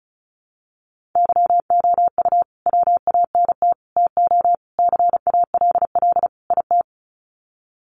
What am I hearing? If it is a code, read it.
Morse code, heard as a keyed tone, about 34 wpm, 700 Hz. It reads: YOU WANT TO CALL IT
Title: Sets of 5 Words Encore2 35wpm